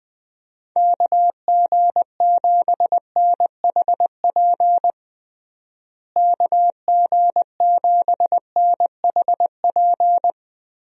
Morse code, heard as a keyed tone, about 20 wpm, 700 Hz. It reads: KG7NHP KG7NHP
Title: USA Call Signs - 6 Characters - Repeated 3x - 20wpm